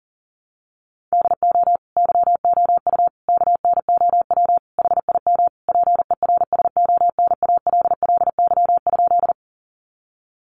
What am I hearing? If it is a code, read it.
Morse code, heard as a keyed tone, about 40 wpm, 700 Hz. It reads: DO YOU KNOW HIM PERSONALLY?
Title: Sentences from Top 1400 Words 40wpm